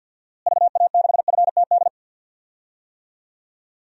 Morse code, heard as a keyed tone, about 46 wpm, 700 Hz. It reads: VA6FTD
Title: DXCC-100 Call Signs - 6 Characters - Repeated 3x - 50wpm